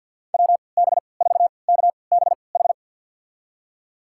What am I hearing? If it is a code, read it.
Morse code, heard as a keyed tone, about 53 wpm, spaced as for 29 wpm, 700 Hz. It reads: WB4KBH
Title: USA Call Signs - 3x Character Spacing - 50wpm